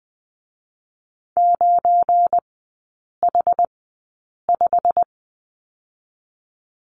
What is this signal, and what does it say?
Morse code, heard as a keyed tone, about 20 wpm, spaced as for 8 wpm, 700 Hz. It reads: 9H5
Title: 172 Lesson 28e - Number 9 - Three Characters 20wpm